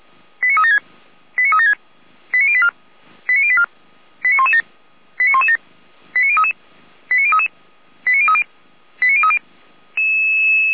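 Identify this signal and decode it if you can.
Five-tone selective call (ZVEI), heard as quick runs of five tones, tones of about 70 ms. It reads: 89376 89376 89084 89084 89117 89117 89033 89033 89030 89030
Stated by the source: ZVEI five-tone selective call received by SDR.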